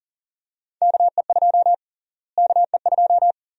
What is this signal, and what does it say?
Morse code, heard as a keyed tone, about 40 wpm, 700 Hz. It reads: KE2 KE2
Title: USA Call Sign Prefixes - Repeated 2x - 40wpm